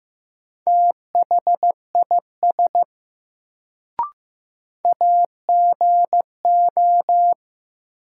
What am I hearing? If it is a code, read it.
Morse code, heard as a keyed tone, about 15 wpm, 700 Hz. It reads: THIS AGO
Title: Sets of 2 Words - New Word Drill - 15wpm